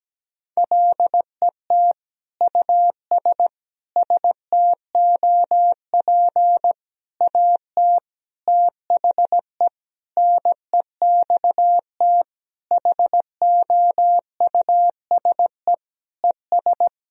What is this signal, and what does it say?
Morse code, heard as a keyed tone, about 17 wpm, 700 Hz. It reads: LET US STOP AT THE NEXT HOUSE ES
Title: Sentences from Top 400 Words and Abbreviations 17wpm